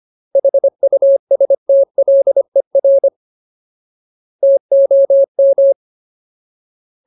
Morse code, heard as a keyed tone, about 25 wpm, 550 Hz. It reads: HUSTLER TOM
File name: Ultimate Word List / 25WPM Ultimate